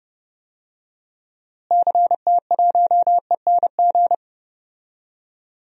Morse code, heard as a keyed tone, about 30 wpm, 700 Hz. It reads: CT1ENG
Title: DXCC-100 Call Signs - 6 Characters - Repeated 2x - 30wpm